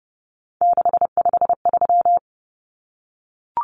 Morse code, heard as a keyed tone, about 30 wpm, 700 Hz. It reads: 653
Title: Numbers - 3-Digit 30wpm